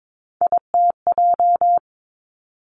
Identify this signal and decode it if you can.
Morse code, heard as a keyed tone, about 22 wpm, 700 Hz. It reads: ITJ
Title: Three Challenging Letter-Number Combinations 22wpm